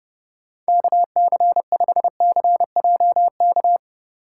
Morse code, heard as a keyed tone, about 29 wpm, 700 Hz. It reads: KC5CJK